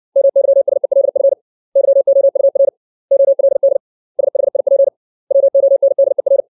Morse code, heard as a keyed tone, about 60 wpm, 550 Hz. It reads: MYSELF YOUR OLD SHIP WONDER
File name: 632 words / 632 Words 60WPM